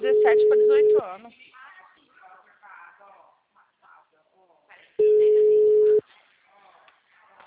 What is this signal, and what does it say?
Ringback tone (Europe, 425 Hz), with speech.